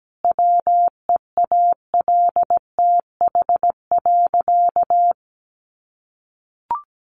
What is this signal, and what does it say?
Morse code, heard as a keyed tone, about 17 wpm, 700 Hz. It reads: WEALTH.
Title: Sentences from Top 1200 Words 17wpm